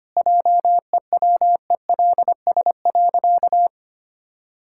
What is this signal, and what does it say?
Morse code, heard as a keyed tone, about 25 wpm, 700 Hz. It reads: JEWELS.
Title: Pangram Challenge v2 - 25wpm